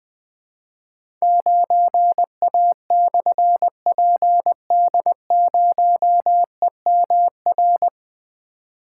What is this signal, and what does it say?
Morse code, heard as a keyed tone, about 20 wpm, 700 Hz. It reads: 9A/PD0EMR
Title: Difficult Call Signs 20wpm